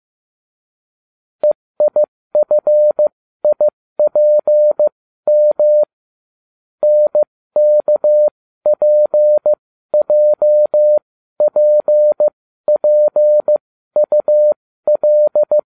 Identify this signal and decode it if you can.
Morse code, heard as a keyed tone, about 15 wpm, spaced as for 13 wpm, 600 Hz. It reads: EIFIPM NKPJPPUL